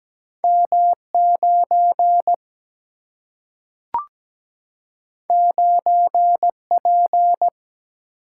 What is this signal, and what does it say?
Morse code, heard as a keyed tone, about 17 wpm, 700 Hz. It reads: M 9 9 P
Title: Single-Letter Single-Number 17wpm